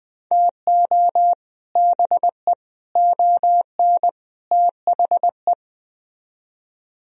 Morse code, heard as a keyed tone, about 20 wpm, 700 Hz. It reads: TO BE ON THE